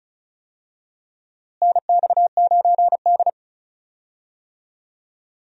Morse code, heard as a keyed tone, about 36 wpm, 700 Hz. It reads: NX9D
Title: USA Call Signs - 4 Characters - Repeated 2x - 35wpm